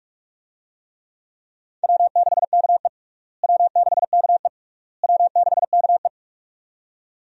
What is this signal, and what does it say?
Morse code, heard as a keyed tone, about 41 wpm, 700 Hz. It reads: W6KE W6KE W6KE
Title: USA Call Signs - 4 Characters - Repeated 3x - 45wpm